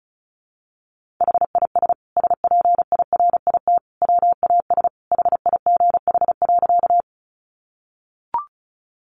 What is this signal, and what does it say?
Morse code, heard as a keyed tone, about 34 wpm, 700 Hz. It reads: HIS SPIRIT WAS HIGH.
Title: Sentences from Top 1400 Words 35wpm